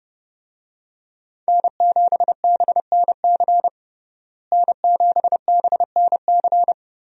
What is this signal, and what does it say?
Morse code, heard as a keyed tone, about 29 wpm, 700 Hz. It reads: N7BNC N7BNC
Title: USA Call Signs - 5 Characters - Repeated 2x - 30wpm